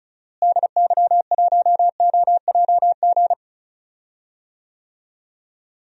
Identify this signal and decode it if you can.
Morse code, heard as a keyed tone, about 34 wpm, 700 Hz. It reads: DY1OJG
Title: DXCC-100 Call Signs - 6 Characters - Repeated 2x - 35wpm